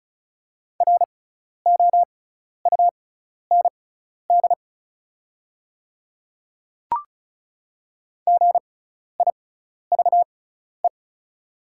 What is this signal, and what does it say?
Morse code, heard as a keyed tone, about 36 wpm, spaced as for 12 wpm, 700 Hz. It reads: ROUND GIVE
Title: 6x Character Spacing - Top 200 Words - 35wpm